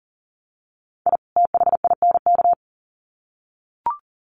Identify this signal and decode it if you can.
Morse code, heard as a keyed tone, about 38 wpm, 700 Hz. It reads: I THINK
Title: Sets of 2 Words 40wpm